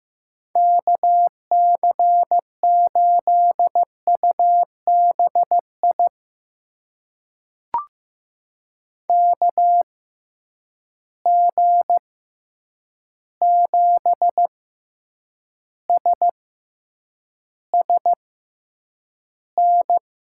Morse code, heard as a keyed tone, about 15 wpm, spaced as for 8 wpm, 700 Hz. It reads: KC8UBI KG7SSN
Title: USA Call Signs - 6 Characters - 6x Character Spacing - 15wpm